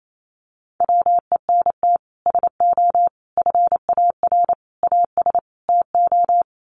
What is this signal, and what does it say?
Morse code, heard as a keyed tone, about 28 wpm, 700 Hz. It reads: WENT SO FAR AS TO
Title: Sets of 5 Words Encore 28wpm